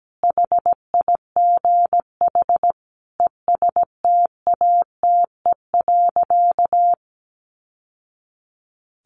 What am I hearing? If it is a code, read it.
Morse code, heard as a keyed tone, about 17 wpm, 700 Hz. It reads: HIGH ESTATE.
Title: Sentences from Top 1200 Words 17wpm